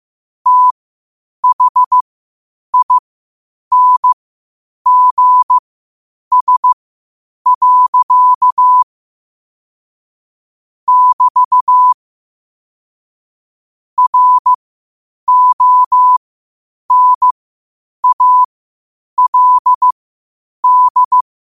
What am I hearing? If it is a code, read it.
Morse code, heard as a keyed tone, about 15 wpm, spaced as for 8 wpm, 1000 Hz. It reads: THINGS. <BT> RONALD
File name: qotdfiles / 8wpm